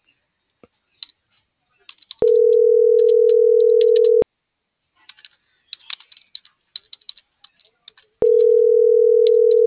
Ringback tone (North America), 440 + 480 Hz.